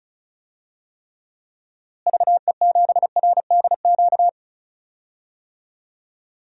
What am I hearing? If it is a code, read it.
Morse code, heard as a keyed tone, about 34 wpm, 700 Hz. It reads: VE7RDQ
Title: DXCC-100 Call Signs - 6 Characters - Repeated 3x - 35wpm